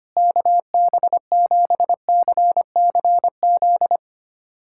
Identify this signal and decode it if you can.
Morse code, heard as a keyed tone, about 25 wpm, 700 Hz. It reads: KB7CCZ
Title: USA Callsign DE Callsign - 6 Characters - 25wpm